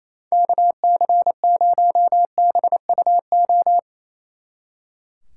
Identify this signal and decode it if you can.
Morse code, heard as a keyed tone, about 34 wpm, 700 Hz. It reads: KC0BUO
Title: Speed Racing - USA 2x3 Call Signs 28wpm